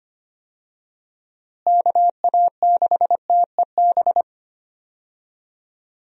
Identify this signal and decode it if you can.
Morse code, heard as a keyed tone, about 25 wpm, 700 Hz. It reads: KA6TEB
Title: USA Call Signs - 6 Characters - Repeated 3x - 25wpm